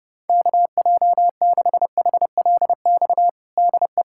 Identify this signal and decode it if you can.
Morse code, heard as a keyed tone, about 30 wpm, 700 Hz. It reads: KJ6HLX DE
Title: USA Callsign DE Callsign - 6 Characters - 30wpm